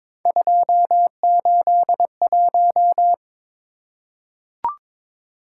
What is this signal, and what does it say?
Morse code, heard as a keyed tone, about 22 wpm, 700 Hz. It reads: 281
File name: practice / 22wpm / Three Letter-Number Combinations - 22wpm